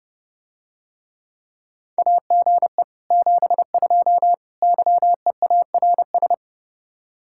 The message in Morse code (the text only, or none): AGE 72 YEARS